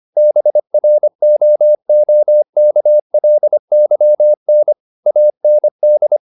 Morse code, heard as a keyed tone, about 25 wpm, 600 Hz. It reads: BROOKLYN AND
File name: the old man and the sea part 1 25wpm